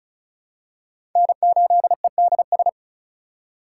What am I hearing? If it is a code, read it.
Morse code, heard as a keyed tone, about 36 wpm, 700 Hz. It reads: N8EDS
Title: USA Call Signs - 4x Character Spacing - 35wpm